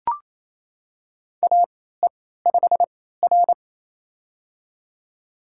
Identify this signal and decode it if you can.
Morse code, heard as a keyed tone, about 28 wpm, spaced as for 16 wpm, 700 Hz. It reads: AE5R
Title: USA Call Signs - 4 Characters - 3x Character Spacing - 28wpm